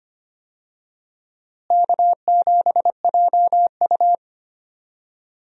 Morse code, heard as a keyed tone, about 25 wpm, 700 Hz. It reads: K7JU
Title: USA Call Signs - 4 Characters - Repeated 2x - 25wpm